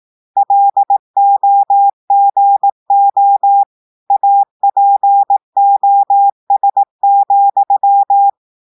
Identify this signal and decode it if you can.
Morse code, heard as a keyed tone, about 18 wpm, 800 Hz. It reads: LOGO APOS,
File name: audios / portugais / portugais texte en clair 2 vitesse 18 mots minutes